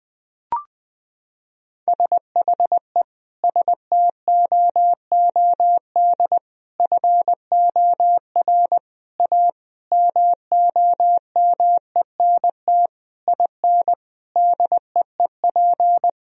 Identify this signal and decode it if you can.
Morse code, heard as a keyed tone, about 20 wpm, 700 Hz. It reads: SHE STOOD FOR A MOMENT IN DEEP
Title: Sentences from Top 1000 Words - Review Part 3 - 20wpm